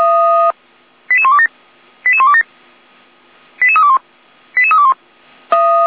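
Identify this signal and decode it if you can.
Five-tone selective call (ZVEI), heard as quick runs of five tones, tones of about 70 ms. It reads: 88127 88127 88321 88321
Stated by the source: ZVEI five-tone call received by SDR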